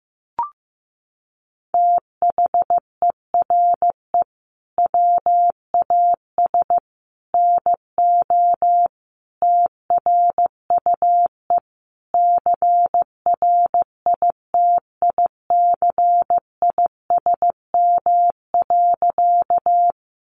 Morse code, heard as a keyed tone, about 15 wpm, 700 Hz. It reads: THERE WAS NO TRUE CRITICISM.
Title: Sentences from Top 1300 Words 15wpm